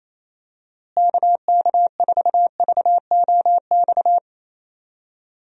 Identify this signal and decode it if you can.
Morse code, heard as a keyed tone, about 28 wpm, 700 Hz. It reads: KK4VOX
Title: USA Call Signs - 6 Characters - Repeated 2x - 28wpm